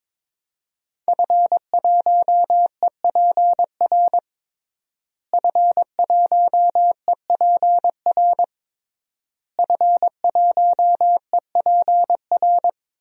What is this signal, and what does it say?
Morse code, heard as a keyed tone, about 22 wpm, 700 Hz. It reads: F1EPR F1EPR F1EPR
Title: DXCC-100 Call Signs - 5 Characters - Repeated 3x - 22wpm